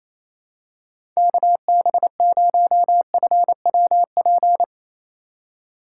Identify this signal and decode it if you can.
Morse code, heard as a keyed tone, about 28 wpm, 700 Hz. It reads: KB0FWP
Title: USA Call Signs - 6 Characters - 3x Character Spacing - 28wpm